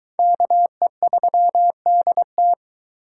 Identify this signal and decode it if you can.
Morse code, heard as a keyed tone, about 20 wpm, 700 Hz. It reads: KE3DT